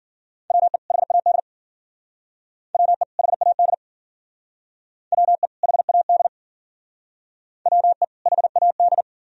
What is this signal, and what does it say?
Morse code, heard as a keyed tone, about 51 wpm, 700 Hz. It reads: WE HAD WE HAD WE HAD WE HAD